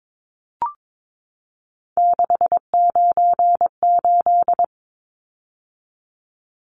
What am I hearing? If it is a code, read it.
Morse code, heard as a keyed tone, about 22 wpm, 700 Hz. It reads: 698
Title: Numbers - 3-Digit 22wpm